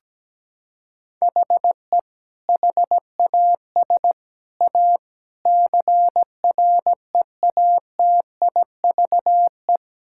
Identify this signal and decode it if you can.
Morse code, heard as a keyed tone, about 17 wpm, 700 Hz. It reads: HE HAS A CREATIVE
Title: Sentences from Top 1000 Words - Review Part 5 - 17wpm